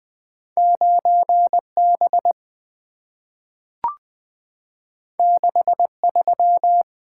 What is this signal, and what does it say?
Morse code, heard as a keyed tone, about 20 wpm, 700 Hz. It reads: 9B 63
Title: Two Letter-Number Combinations - 20wpm